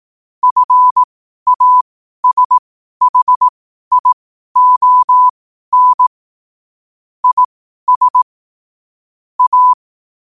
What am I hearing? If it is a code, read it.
Morse code, heard as a keyed tone, about 18 wpm, spaced as for 12 wpm, 1000 Hz. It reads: FASHION IS A
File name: qotdfiles / 12wpm